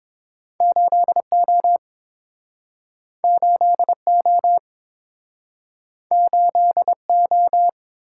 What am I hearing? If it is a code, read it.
Morse code, heard as a keyed tone, about 30 wpm, 700 Hz. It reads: 8O 8O 8O